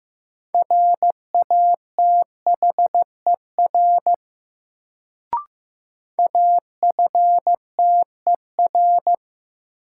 Morse code, heard as a keyed tone, about 16 wpm, 700 Hz. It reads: RATHER AFTER
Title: Sets of 2 Words - New Word Drill - 15wpm